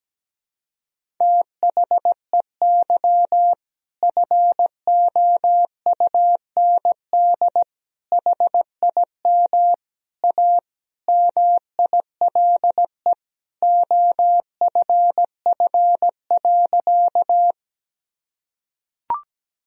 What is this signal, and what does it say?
Morse code, heard as a keyed tone, about 17 wpm, 700 Hz. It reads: THEY FOUND HIM A MILE OFF.
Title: Sentences from Top 1000 Words - Review Part 1 - 17wpm